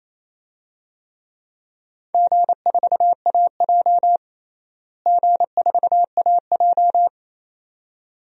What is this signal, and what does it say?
Morse code, heard as a keyed tone, about 28 wpm, 700 Hz. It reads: G4AJ G4AJ
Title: DXCC-100 Call Signs - 4 Characters - Repeated 2x - 28wpm